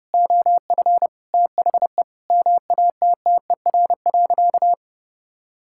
Morse code, heard as a keyed tone, about 30 wpm, 700 Hz. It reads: OF THE MATTER.
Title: Sentences from Top 1000 Words - Review Part 6 - 30wpm